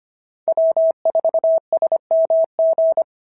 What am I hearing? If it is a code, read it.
Morse code, heard as a keyed tone, about 25 wpm, 690 Hz. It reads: W4SMG
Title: SOTA QSO Practice 25wpm